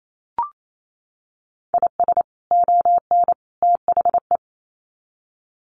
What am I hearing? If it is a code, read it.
Morse code, heard as a keyed tone, about 28 wpm, 700 Hz. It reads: IS ON THE
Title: Sets of 3 Words 28wpm